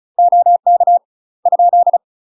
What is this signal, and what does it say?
Morse code, heard as a keyed tone, about 36 wpm, 700 Hz. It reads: OK ?
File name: UK MASTER QSOs TRAINING / 35WPM / UK MASTER QSOs 0030